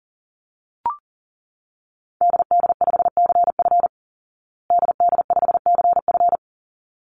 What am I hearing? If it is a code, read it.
Morse code, heard as a keyed tone, about 40 wpm, 700 Hz. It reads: DD5CF DD5CF
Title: Call Signs Repeated 3x - 40wpm